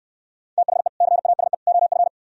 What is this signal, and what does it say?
Morse code, heard as a keyed tone, about 67 wpm, 700 Hz. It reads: THE CASE OF